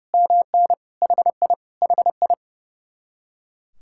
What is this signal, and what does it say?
Morse code, heard as a keyed tone, about 30 wpm, 700 Hz. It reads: MN HI HI
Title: Speed-Racing Common QSO Phrases - 30wpm